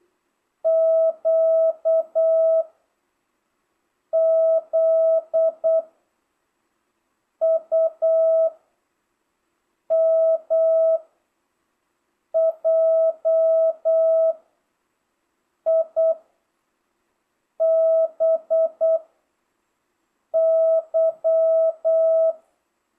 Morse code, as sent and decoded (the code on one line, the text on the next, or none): --.- / --.. / ..- / -- / .--- / .. / -... / -.--
Q Z U M J I B Y